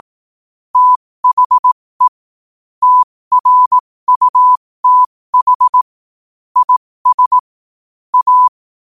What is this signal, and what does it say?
Morse code, heard as a keyed tone, about 18 wpm, spaced as for 15 wpm, 1000 Hz. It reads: THE TRUTH IS A
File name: qotdfiles / 15wpm